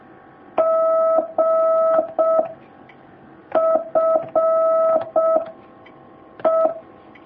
Morse code, heard as a keyed tone, about 6 wpm, 650 Hz. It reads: G F E
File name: jh8cbh / 2019cw